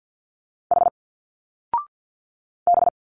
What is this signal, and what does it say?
Morse code, heard as a keyed tone, about 49 wpm, 700 Hz. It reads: H B
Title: Single Letters 50wpm